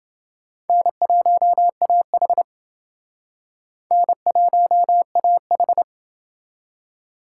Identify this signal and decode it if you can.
Morse code, heard as a keyed tone, about 28 wpm, 700 Hz. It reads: N1AH N1AH